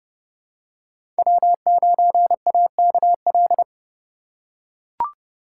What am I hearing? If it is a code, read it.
Morse code, heard as a keyed tone, about 32 wpm, 700 Hz. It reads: W9AKL
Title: USA Call Signs - 5 Characters - 2x Character Spacing - 30wpm